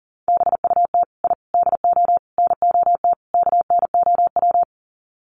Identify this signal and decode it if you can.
Morse code, heard as a keyed tone, about 40 wpm, 700 Hz. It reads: BUT I DO NOT KNOW
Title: Sets of 5 Words 40wpm